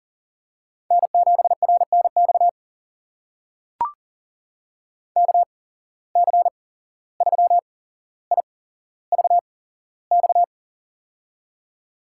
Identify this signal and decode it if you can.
Morse code, heard as a keyed tone, about 39 wpm, spaced as for 16 wpm, 700 Hz. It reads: N7RNX KC3IVX
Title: USA Call Signs - 8x Character Spacing - 40wpm